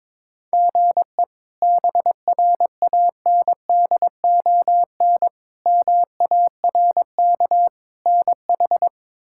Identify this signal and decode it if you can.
Morse code, heard as a keyed tone, about 22 wpm, 700 Hz. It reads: GE BRANDON MARK NH
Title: K1USN SST QSO Practice - 22wpm